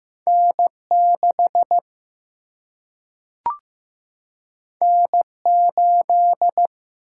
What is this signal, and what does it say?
Morse code, heard as a keyed tone, about 15 wpm, 700 Hz. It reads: N 6 N 8
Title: Single-Letter Single-Number 15wpm